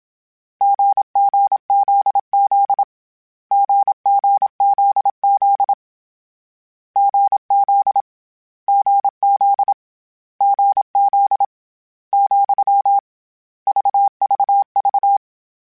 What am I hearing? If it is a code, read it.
Morse code, heard as a keyed tone, about 27 wpm, 800 Hz. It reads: GGZZ GGZZ GZ GZ GZ , VVV
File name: Difficult Combos / Difficult Combos-mp3s / GZ GZ7 @25